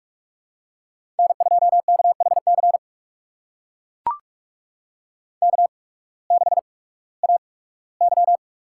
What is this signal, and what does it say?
Morse code, heard as a keyed tone, about 44 wpm, spaced as for 21 wpm, 700 Hz. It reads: N2KHC K6AY
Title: USA Call Signs - 8x Character Spacing - 45wpm